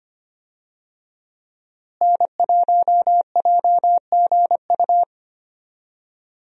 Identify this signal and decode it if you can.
Morse code, heard as a keyed tone, about 25 wpm, 700 Hz. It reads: N1JGU
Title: USA Call Signs - 5 Characters - Repeated 2x - 25wpm